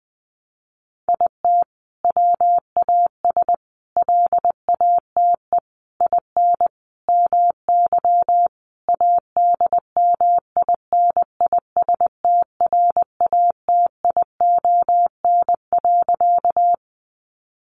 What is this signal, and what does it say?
Morse code, heard as a keyed tone, about 20 wpm, 700 Hz. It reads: IT WAS LATE IN MY ADMINISTRATION.
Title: Sentences from Top 1300 Words 20wpm